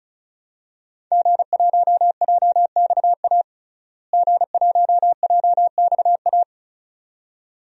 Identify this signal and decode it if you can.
Morse code, heard as a keyed tone, about 34 wpm, 700 Hz. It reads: G1JXA G1JXA
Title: DXCC-100 Call Signs - 5 Characters - Repeated 2x - 35wpm